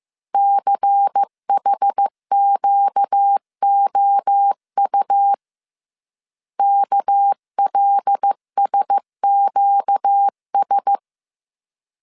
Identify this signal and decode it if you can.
Morse code, heard as a keyed tone, about 15 wpm, 800 Hz. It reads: CHQOU KLSQS
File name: CW / DET / 1 6 1500C S A 14wpm 2921